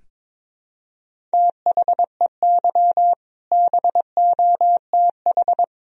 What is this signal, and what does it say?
Morse code, heard as a keyed tone, about 22 wpm, 700 Hz. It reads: THEY BOTH
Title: Sentences from Top 1000 Words - Review Part 1 - 22wpm